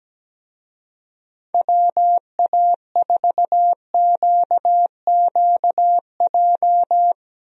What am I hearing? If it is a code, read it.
Morse code, heard as a keyed tone, about 17 wpm, 700 Hz. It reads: WA4QQJ
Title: USA Call Signs - 6 Characters - Repeated 2x - 17wpm